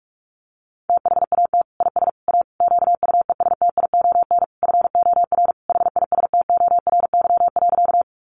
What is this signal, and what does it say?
Morse code, heard as a keyed tone, about 44 wpm, 700 Hz. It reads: THAT IS A QUESTION FOR HISTORY.
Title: Sentences from Top 600 Words 45wpm